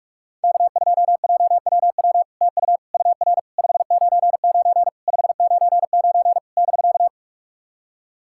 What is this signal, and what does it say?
Morse code, heard as a keyed tone, about 46 wpm, 700 Hz. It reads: K2JWW TU UR 599 599 <BK>
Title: SOTA QSO Practice 45wpm